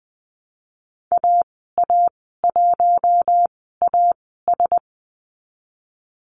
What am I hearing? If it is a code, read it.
Morse code, heard as a keyed tone, about 19 wpm, spaced as for 14 wpm, 700 Hz. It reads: AA1AS
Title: 062 Lesson 11g - Letter h - Call Signs 20wpm